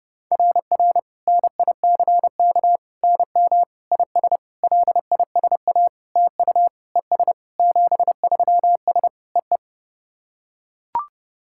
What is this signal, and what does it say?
Morse code, heard as a keyed tone, about 30 wpm, 690 Hz. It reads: RR NICK NM IS LISA TU ES 73S EE
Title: SOTA QSO Practice 30wpm